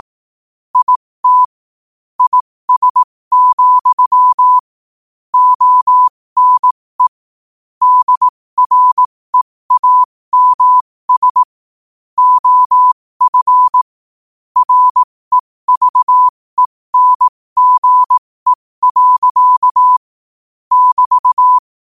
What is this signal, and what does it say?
Morse code, heard as a keyed tone, about 18 wpm, spaced as for 15 wpm, 1000 Hz. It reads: IT IS, ONE DREAMS OF REVENGE. <BT>
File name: qotdfiles / 15wpm